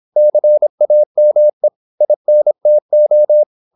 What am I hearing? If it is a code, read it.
Morse code, heard as a keyed tone, about 26 wpm, 600 Hz. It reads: CAME INTO
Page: Never 26wpm | CW med Gnister